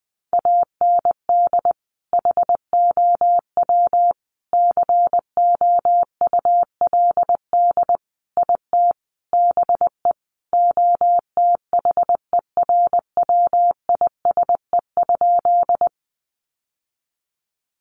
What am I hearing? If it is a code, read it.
Morse code, heard as a keyed tone, about 20 wpm, 700 Hz. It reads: AND HOW COULD IT BE OTHERWISE?
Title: Sentences from Top 1200 Words 20wpm